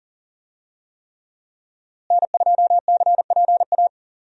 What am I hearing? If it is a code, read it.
Morse code, heard as a keyed tone, about 40 wpm, 700 Hz. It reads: N2CPA
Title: USA Call Signs - 5 Characters - Repeated 3x - 40wpm